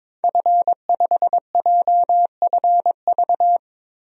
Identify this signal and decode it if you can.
Morse code, heard as a keyed tone, about 22 wpm, 700 Hz. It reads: F5JFV
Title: DXCC-100 Call Signs - 5 Characters - Repeated 2x - 22wpm